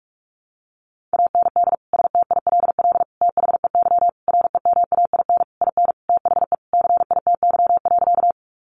Morse code, heard as a keyed tone, about 44 wpm, 700 Hz. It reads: AND STILL THEY REMAIN IN THE CITY.
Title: Sentences from Top 1200 Words 45wpm